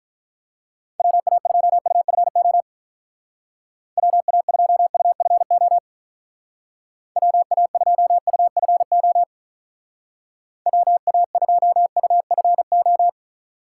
Morse code, heard as a keyed tone, about 44 wpm, 700 Hz. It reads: WA2UFO WA2UFO WA2UFO WA2UFO
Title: Speed Racing - USA 2x3 Call Signs 35wpm